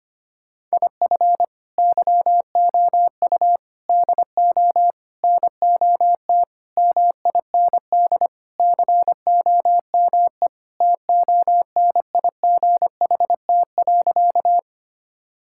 Morse code, heard as a keyed tone, about 25 wpm, 700 Hz. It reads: IF YOU DO NOT MIND COME TONIGHT.
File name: practice / 25wpm / Sentences with 4-Word Phrases - 25wpm